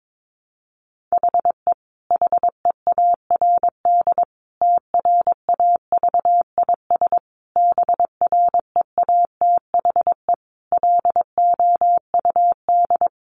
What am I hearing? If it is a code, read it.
Morse code, heard as a keyed tone, about 22 wpm, 700 Hz. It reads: HE HEARD TRAVIS BREATHE LOUD
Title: A Sound Of Thunder 22wpm